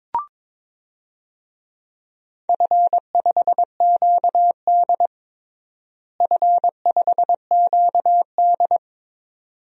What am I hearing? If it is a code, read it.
Morse code, heard as a keyed tone, about 22 wpm, 700 Hz. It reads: F5QD F5QD
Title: DXCC-100 Call Signs - 4 Characters - Repeated 2x - 22wpm